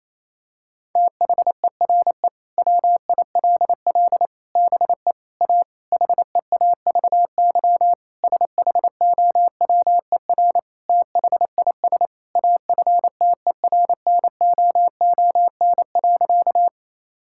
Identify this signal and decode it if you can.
Morse code, heard as a keyed tone, about 28 wpm, 700 Hz. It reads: THERE WILL BE A HEAVY SHOWER THIS AFTERNOON.
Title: Sentences from Top 1600 Words 28wpm